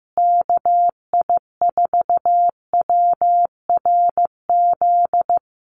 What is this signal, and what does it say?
Morse code, heard as a keyed tone, about 15 wpm, 700 Hz. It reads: KI4WRZ
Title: USA Call Signs Repeated 3x 15wpm